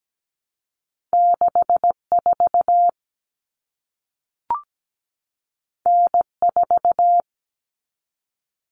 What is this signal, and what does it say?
Morse code, heard as a keyed tone, about 17 wpm, 700 Hz. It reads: 64 N4
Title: Common Cut Numbers - 2-Digit 17wpm